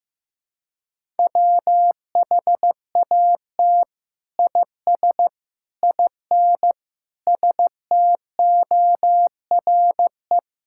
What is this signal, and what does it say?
Morse code, heard as a keyed tone, about 15 wpm, 700 Hz. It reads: WHAT IS IN STORE
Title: Sentences from Top 1000 Words - Review Part 5 - 15wpm